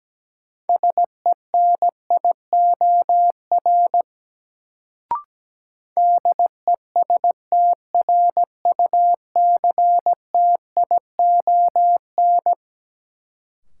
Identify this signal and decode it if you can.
Morse code, heard as a keyed tone, about 17 wpm, 700 Hz. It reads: SENIOR DESTRUCTION
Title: Sets of 5 Words Encore - New Word Drill - 17wpm